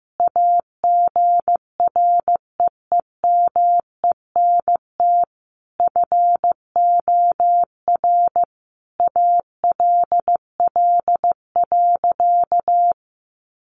Morse code, heard as a keyed tone, about 15 wpm, 700 Hz. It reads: AGREEMENT FOR ALL.
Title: Sentences from Top 1500 Words 15wpm